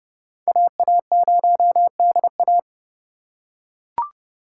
Morse code, heard as a keyed tone, about 30 wpm, 700 Hz. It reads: AA0DA
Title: USA Call Signs - 5 Characters - 3x Character Spacing - 30wpm